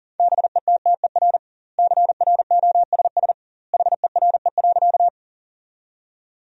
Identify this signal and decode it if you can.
Morse code, heard as a keyed tone, about 40 wpm, 700 Hz. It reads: BETTER CROSS HERE.
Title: 2x Word-Spacing - Sentences from Top 400 Words - 40wpm